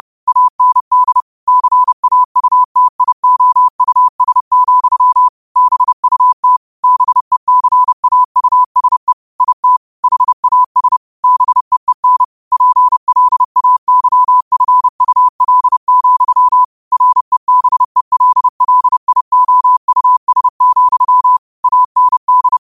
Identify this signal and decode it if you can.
Morse code, heard as a keyed tone, about 30 wpm, 1000 Hz. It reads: AND CAUTIOUS, BUT BECAUSE IT HAS BEEN PLAYFUL, REBELLIOUS, AND
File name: qotdfiles / 30wpm